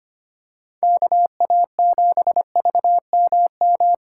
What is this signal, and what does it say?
Morse code, heard as a keyed tone, about 25 wpm, 700 Hz. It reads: KA7VMM
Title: USA Call Signs - 6 Characters - 3x Character Spacing - 25wpm